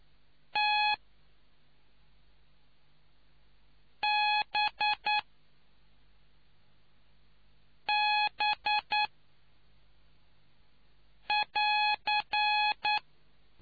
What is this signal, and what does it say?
Morse code, heard as a keyed tone, about 9 wpm, 840 Hz. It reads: T B B <AR>